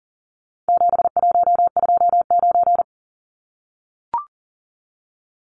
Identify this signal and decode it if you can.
Morse code, heard as a keyed tone, about 39 wpm, 700 Hz. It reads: 7129
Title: Numbers - 4-Digit 40wpm